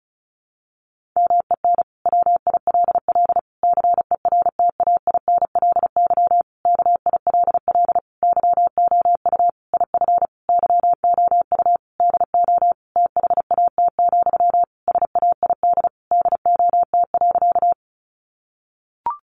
Morse code, heard as a keyed tone, about 34 wpm, 700 Hz. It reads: MEN WILL CERTAINLY KILL YOU IF YOU DO THAT, SAID DOT.
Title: Sentences from Top 1500 Words 35wpm